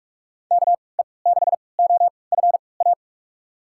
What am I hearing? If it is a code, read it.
Morse code, heard as a keyed tone, about 46 wpm, spaced as for 26 wpm, 700 Hz. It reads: KE6OFA
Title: USA Call Signs - 3x Character Spacing - 45wpm